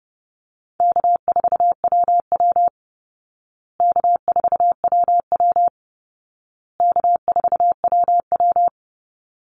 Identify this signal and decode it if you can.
Morse code, heard as a keyed tone, about 30 wpm, 700 Hz. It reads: K4WW K4WW K4WW
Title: Easy Call Signs Repeated 3x - 30wpm